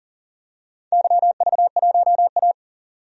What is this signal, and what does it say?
Morse code, heard as a keyed tone, about 40 wpm, 700 Hz. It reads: YV1A
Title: DXCC-100 Call Signs - 4 Characters - 40wpm